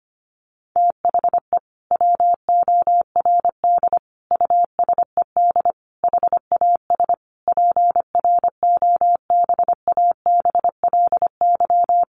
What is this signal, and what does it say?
Morse code, heard as a keyed tone, about 25 wpm, 700 Hz. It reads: THE WORD USED HAS PROBABLY
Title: Sentences from Top 1100 Words 25wpm